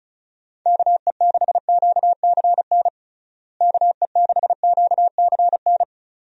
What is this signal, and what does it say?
Morse code, heard as a keyed tone, about 36 wpm, 700 Hz. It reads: KE6QCN KE6QCN
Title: USA Call Signs - 6 Characters - Repeated 2x - 35wpm